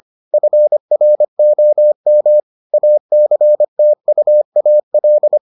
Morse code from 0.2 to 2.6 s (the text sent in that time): FROM